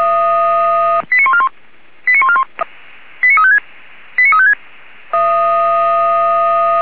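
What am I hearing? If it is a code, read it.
Five-tone selective call (ZVEI), heard as quick runs of five tones, tones of about 70 ms. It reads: 89141 89141 89356 89356
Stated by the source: ZVEI five-tone selective call 89355 received by SDR, logged by Feuerwehr Orlamünde fire brigade.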